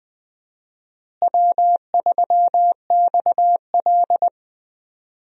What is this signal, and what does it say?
Morse code, heard as a keyed tone, about 20 wpm, 700 Hz. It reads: W3XL
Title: USA Call Signs - 4 Characters - Repeated 3x - 20wpm